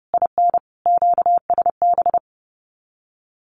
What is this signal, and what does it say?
Morse code, heard as a keyed tone, about 30 wpm, 700 Hz. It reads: IN QSB
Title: Common QSO Phrases 30wpm